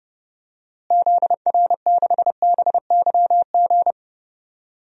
Morse code, heard as a keyed tone, about 30 wpm, 700 Hz. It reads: ZR6BYG
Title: DXCC-100 Call Signs - 6 Characters - Repeated 2x - 30wpm